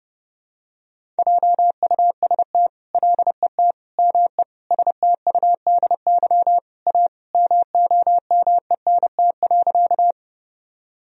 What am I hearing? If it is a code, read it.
Morse code, heard as a keyed tone, about 30 wpm, 700 Hz. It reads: JUST LET ME STUDY A MOMENT.
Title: Sentences from Top 1000 Words - Review Part 1 - 30wpm